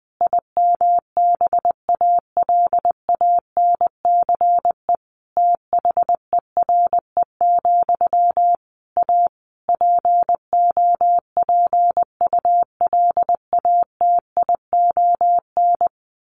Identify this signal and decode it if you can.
Morse code, heard as a keyed tone, about 20 wpm, 700 Hz. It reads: IMBALANCE THERE, A POPULATION
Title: A Sound Of Thunder 20wpm